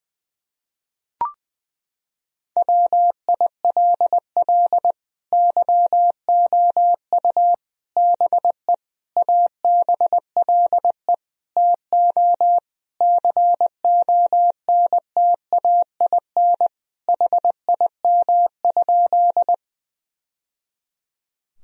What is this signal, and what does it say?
Morse code, heard as a keyed tone, about 21 wpm, 700 Hz. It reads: WILL YOU BE ABLE TO CONTAIN HIM?
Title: Sentences from Top 1000 Words - Review Part 2 - 20wpm